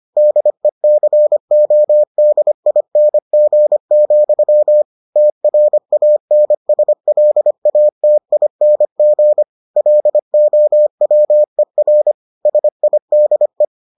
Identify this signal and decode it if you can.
Morse code, heard as a keyed tone, about 25 wpm, 600 Hz. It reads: DECODING, TRANSLATING LOWER SIDE